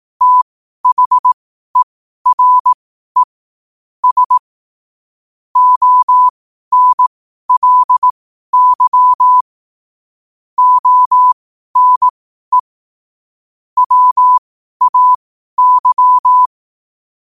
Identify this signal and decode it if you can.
Morse code, heard as a keyed tone, about 18 wpm, spaced as for 12 wpm, 1000 Hz. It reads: THERE S ONLY ONE WAY
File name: qotdfiles / 12wpm